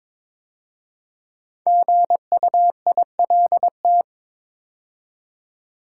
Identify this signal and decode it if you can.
Morse code, heard as a keyed tone, about 22 wpm, 700 Hz. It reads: GUILT
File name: 5 Letter Words - First 3 Letters Given - 22wpm